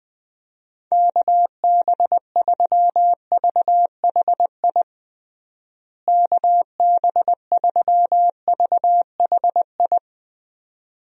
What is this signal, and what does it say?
Morse code, heard as a keyed tone, about 21 wpm, 700 Hz. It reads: KB3VHI KB3VHI
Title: USA Call Signs - 6 Characters - Repeated 2x - 20wpm